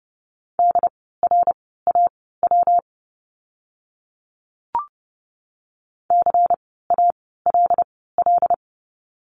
Morse code, heard as a keyed tone, about 30 wpm, spaced as for 17 wpm, 700 Hz. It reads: DRAW CALL
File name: course / 30:10wpm / 115 Lesson 19f - Letter w - Words 30wpm